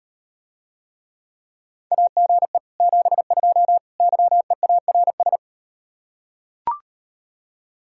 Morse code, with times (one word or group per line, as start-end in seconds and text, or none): 1.9-2.6 AGE
2.8-3.8 72
4.0-5.4 YEARS